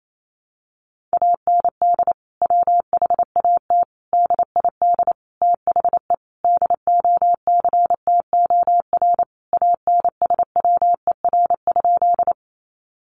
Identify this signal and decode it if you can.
Morse code, heard as a keyed tone, about 28 wpm, 700 Hz. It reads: AND WHAT DID THE DOCTOR ANSWER?
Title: Sentences from Top 1500 Words 28wpm